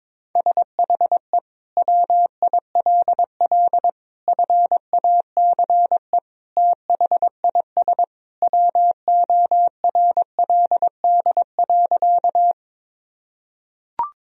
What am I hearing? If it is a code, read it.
Morse code, heard as a keyed tone, about 22 wpm, 700 Hz. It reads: SHE WILL FACE THIS WORLD.
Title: Sentences from Top 1000 Words - Review Part 2 - 22wpm